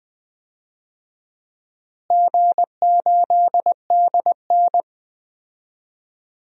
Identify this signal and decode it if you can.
Morse code, heard as a keyed tone, about 20 wpm, 700 Hz. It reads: G8DN
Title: DXCC-100 Call Signs - 4 Characters - Repeated 3x - 20wpm